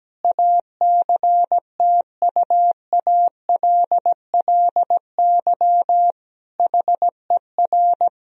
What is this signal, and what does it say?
Morse code, heard as a keyed tone, about 17 wpm, 700 Hz. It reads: ACTUALLY HER
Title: Sentences from Top 1000 Words - Review Part 4 - 17wpm